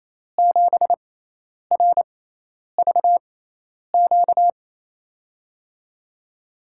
Morse code, heard as a keyed tone, about 28 wpm, spaced as for 10 wpm, 700 Hz. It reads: 7RVQ
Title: USA Call Signs - 6 Characters - 6x Character Spacing - 28wpm